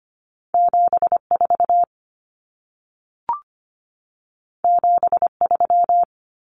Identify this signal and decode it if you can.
Morse code, heard as a keyed tone, about 25 wpm, 700 Hz. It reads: 74 73
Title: Common Cut Numbers - 2-Digit 25wpm